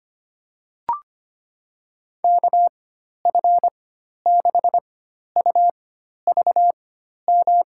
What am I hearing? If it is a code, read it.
Morse code, heard as a keyed tone, about 25 wpm, spaced as for 12 wpm, 700 Hz. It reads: KF6UVM
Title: USA Call Signs - 6 Characters - 4x Character Spacing - 25wpm